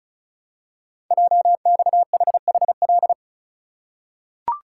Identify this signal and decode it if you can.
Morse code, heard as a keyed tone, about 36 wpm, 700 Hz. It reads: JXHHL
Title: Code Groups - 5 Letters - 35wpm